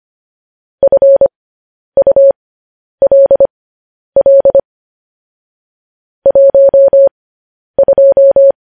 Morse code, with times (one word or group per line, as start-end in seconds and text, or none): 0.8-4.6 FULL
6.3-8.5 12